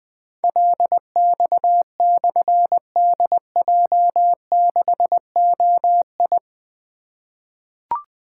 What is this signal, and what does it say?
Morse code, heard as a keyed tone, about 20 wpm, 700 Hz. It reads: LX/DJ6OI
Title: Difficult Call Signs 20wpm